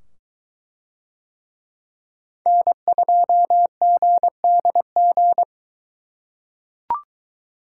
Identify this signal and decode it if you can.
Morse code, heard as a keyed tone, about 23 wpm, 700 Hz. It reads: N2GDG